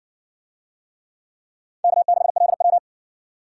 Morse code, heard as a keyed tone, about 60 wpm, 700 Hz. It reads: K6LW